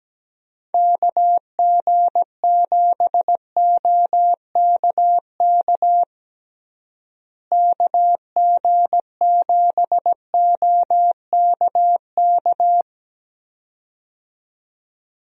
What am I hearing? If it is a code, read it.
Morse code, heard as a keyed tone, about 17 wpm, 700 Hz. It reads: KG7OKK KG7OKK
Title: USA Call Signs - 6 Characters - Repeated 2x - 17wpm